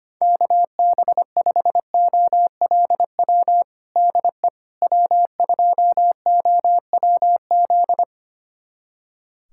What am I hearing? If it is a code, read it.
Morse code, heard as a keyed tone, about 25 wpm, 700 Hz. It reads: KB5OLW DE W2OWZ
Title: USA Callsign DE Callsign 25wpm